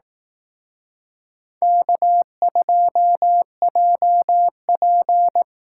Morse code, heard as a keyed tone, about 18 wpm, 700 Hz. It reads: K2JP
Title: Speed Racing - USA 1x2 Call Signs 15wpm